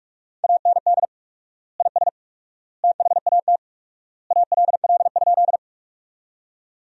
Morse code, heard as a keyed tone, about 42 wpm, 700 Hz. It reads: AND IS THAT ALL?
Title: 4x Word-Spacing - Sentences from Top 100 Words - 45wpm